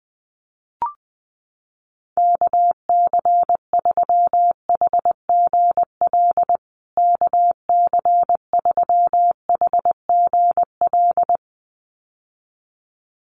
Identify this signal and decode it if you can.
Morse code, heard as a keyed tone, about 20 wpm, 700 Hz. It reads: KC3HGL KC3HGL
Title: USA Call Signs Repeated 2x 20wpm